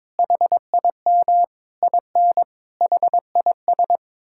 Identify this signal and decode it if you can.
Morse code, heard as a keyed tone, about 22 wpm, 700 Hz. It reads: HIM IN HIS
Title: Sentences from Top 1600 Words 22wpm